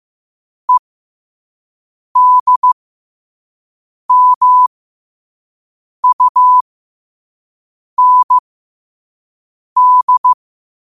Morse code, heard as a keyed tone, about 15 wpm, spaced as for 5 wpm, 1000 Hz. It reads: EDMUND
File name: 5wpm